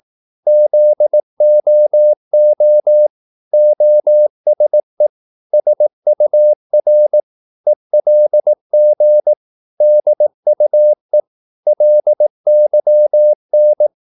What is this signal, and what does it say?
Morse code, heard as a keyed tone, about 18 wpm, 600 Hz. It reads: ZOO OSE SUR ELG DUE LYN
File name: Korte ord DK 18wpm